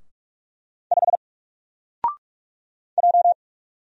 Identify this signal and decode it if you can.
Morse code, heard as a keyed tone, about 45 wpm, 700 Hz. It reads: 5 J
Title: Single Letter-Number Alt 45wpm